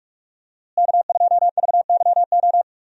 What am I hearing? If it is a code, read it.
Morse code, heard as a keyed tone, about 46 wpm, 700 Hz. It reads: K2VYO
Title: USA Call Signs - 5 Characters - Repeated 3x - 45wpm